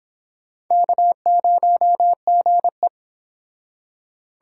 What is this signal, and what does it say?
Morse code, heard as a keyed tone, about 26 wpm, 700 Hz. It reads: K0GE